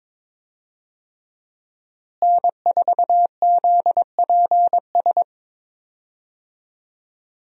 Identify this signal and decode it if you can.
Morse code, heard as a keyed tone, about 22 wpm, 700 Hz. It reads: N4ZPS
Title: USA Call Signs - 5 Characters - Repeated 3x - 22wpm